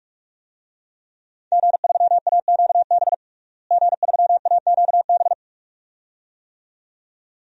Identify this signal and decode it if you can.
Morse code, heard as a keyed tone, about 42 wpm, 700 Hz. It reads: G3AQB G3AQB
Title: DXCC-100 Call Signs - 5 Characters - Repeated 2x - 45wpm